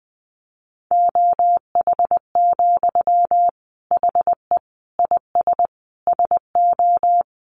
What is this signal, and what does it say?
Morse code, heard as a keyed tone, about 20 wpm, 700 Hz. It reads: OH, HE IS SO
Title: Sentences from Top 500 Words 20wpm